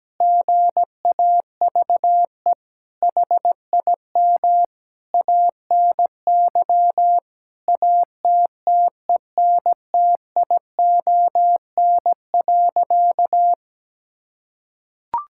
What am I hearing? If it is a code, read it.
Morse code, heard as a keyed tone, about 17 wpm, 700 Hz. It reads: GAVE HIM ANY ATTENTION.
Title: Sentences from Top 1000 Words - Review Part 6 - 17wpm